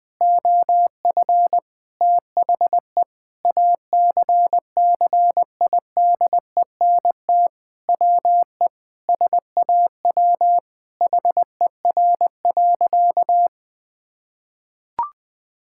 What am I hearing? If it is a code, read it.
Morse code, heard as a keyed tone, about 20 wpm, 700 Hz. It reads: OF THE ACCIDENT WE SAW HER.
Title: Sentences with 4-Word Phrases - Spoken Beginning - 20wpm